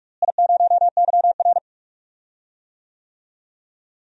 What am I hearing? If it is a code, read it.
Morse code, heard as a keyed tone, about 43 wpm, 700 Hz. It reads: I0YR
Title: DXCC-100 Call Signs - 4 Characters - Repeated 3x - 45wpm